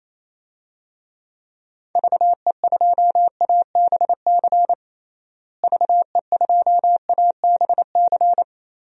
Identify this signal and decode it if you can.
Morse code, heard as a keyed tone, about 27 wpm, 700 Hz. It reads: VE2ABC VE2ABC
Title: DXCC-100 Call Signs - 6 Characters - Repeated 3x - 28wpm